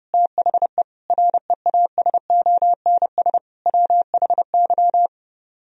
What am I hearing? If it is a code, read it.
Morse code, heard as a keyed tone, about 33 wpm, 700 Hz. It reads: THE REASONS WHY